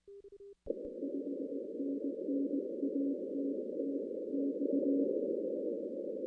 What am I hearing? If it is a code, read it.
Morse code, heard as a keyed tone, about 19 wpm, 280 Hz. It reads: 5NN ATTX
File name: 2016 arrl dx cw / 2016-ARRL-DX-CW-K5TR- - 0-26